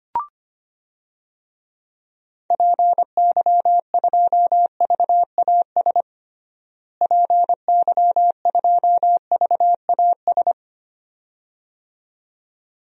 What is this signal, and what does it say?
Morse code, heard as a keyed tone, about 25 wpm, 700 Hz. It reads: PY2VAS PY2VAS
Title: DXCC-100 Call Signs - 6 Characters - Repeated 2x - 25wpm